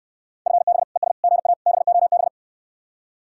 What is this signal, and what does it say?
Morse code, heard as a keyed tone, about 68 wpm, 700 Hz. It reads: FB ES GA BOB